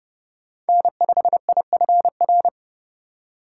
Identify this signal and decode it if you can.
Morse code, heard as a keyed tone, about 31 wpm, 700 Hz. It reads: N5IFR
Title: USA Call Signs - 5 Characters - 3x Character Spacing - 30wpm